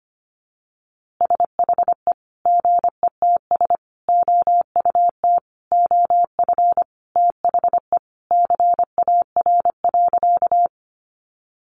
Morse code, heard as a keyed tone, about 25 wpm, 700 Hz. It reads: SHE GETS OUT OF THE CAR.
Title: Easy Phrases 25wpm